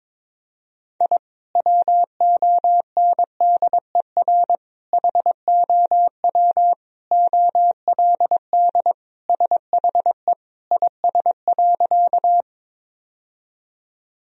Morse code, heard as a keyed tone, about 22 wpm, 700 Hz. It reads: I WONDER HOW OLD SHE IS.
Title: Sentences from Top 1000 Words - Review Part 3 - 22wpm